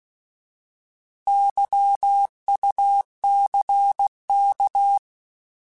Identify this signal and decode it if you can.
Morse code, heard as a keyed tone, about 16 wpm, 800 Hz. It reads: YUCK
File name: Alphabetical / Alphabetical mp3s / Ys @15